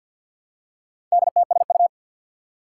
Morse code, heard as a keyed tone, about 51 wpm, 700 Hz. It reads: DTSU
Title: Four Challenging Letter-Number Combinations 50wpm